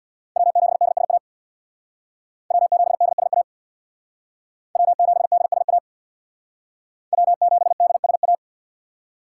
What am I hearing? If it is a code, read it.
Morse code, heard as a keyed tone, about 66 wpm, 700 Hz. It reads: W7DSA W7DSA W7DSA W7DSA